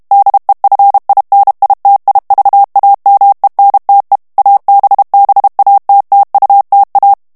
Morse code, heard as a keyed tone, about 32 wpm, 800 Hz. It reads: DEFINITIVAMENTE ABBATTUTA
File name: morse mp3 / ITALIEN texte clair 1 vitesse 30 mots minute